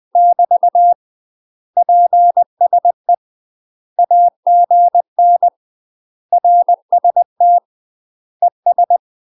Morse code, heard as a keyed tone, about 20 wpm, 700 Hz. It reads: <BT> PSE AGN RST ES
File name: UK MASTER QSOs TRAINING / 20WPM / UK MASTER QSOs 0051